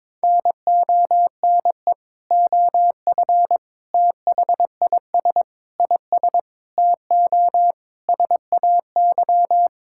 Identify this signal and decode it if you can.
Morse code, heard as a keyed tone, about 23 wpm, 700 Hz. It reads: NONE OF THIS IS TO SAY
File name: Sentences with 7-Word Phrases - 22wpm